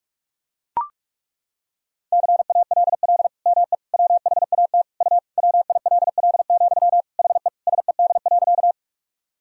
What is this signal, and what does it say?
Morse code, heard as a keyed tone, about 41 wpm, 700 Hz. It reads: CALL ME WHAT U WILL, HE SED.
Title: Sentences from Top 200 Words and Abbreviations 45wpm